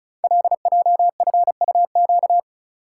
Morse code, heard as a keyed tone, about 36 wpm, 690 Hz. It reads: LJFUQ
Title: Code Groups - 5 Letters - 35wpm